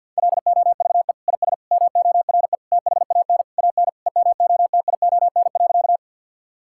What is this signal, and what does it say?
Morse code, heard as a keyed tone, about 50 wpm, 700 Hz. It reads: LOVE IS MORE THAN AN EMOTION.
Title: Sentences from Top 1600 Words 50wpm